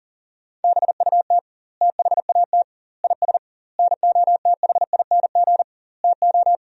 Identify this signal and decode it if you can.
Morse code, heard as a keyed tone, about 40 wpm, 700 Hz. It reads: BUT THAT IS NOTHING TO
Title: 2x Word-Spacing - Sentences from Top 500 Words - 40wpm